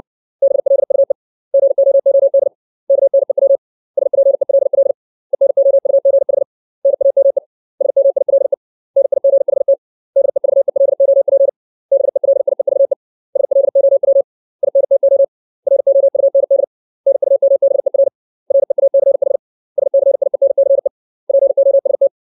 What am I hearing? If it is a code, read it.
Morse code, heard as a keyed tone, about 60 wpm, 550 Hz. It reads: BLUE GOOD KNEW SPELL ENOUGH NAME SMILE NIGHT DEVELOP BELIEVE SLOW INTO ROUND NUMBER REACH SCIENCE POST
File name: 632 words / 632 Words 60WPM